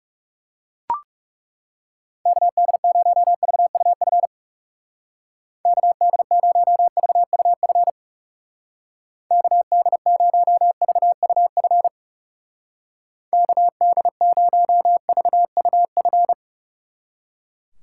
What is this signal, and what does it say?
Morse code, heard as a keyed tone, about 36 wpm, 700 Hz. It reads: KD0VUF KD0VUF KD0VUF KD0VUF
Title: Speed Racing - USA 2x3 Call Signs 30wpm